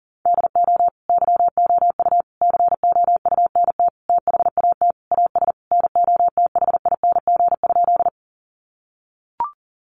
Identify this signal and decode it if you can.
Morse code, heard as a keyed tone, about 40 wpm, 700 Hz. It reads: DO YOU COUNT THAT AS NOTHING?
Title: Sentences from Top 1500 Words 40wpm